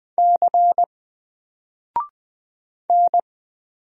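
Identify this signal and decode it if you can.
Morse code, heard as a keyed tone, about 20 wpm, 700 Hz. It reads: C N
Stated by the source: Single letter and number Morse drill at 20 wpm, from N4LCD.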